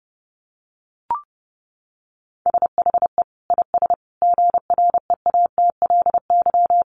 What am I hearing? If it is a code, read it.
Morse code, heard as a keyed tone, about 30 wpm, 700 Hz. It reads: SHE IS GREATLY
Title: Sentences from Top 1400 Words 30wpm